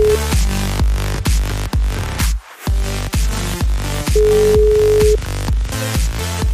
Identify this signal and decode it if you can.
Ringback tone (Europe, 425 Hz), with noise.